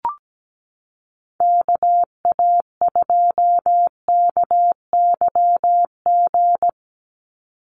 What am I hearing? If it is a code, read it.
Morse code, heard as a keyed tone, about 17 wpm, 700 Hz. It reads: KA2KYG
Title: USA Call Signs 17wpm